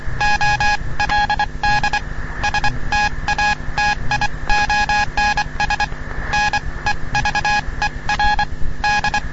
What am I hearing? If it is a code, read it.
Morse code, heard as a keyed tone, about 24 wpm, spaced as for 21 wpm, 830 Hz, with noise. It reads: OLD STATIONS NEVER D
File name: europe / bel / bel-265-ostooestenderadioclg-g-240-gldlsndsend-1997-ll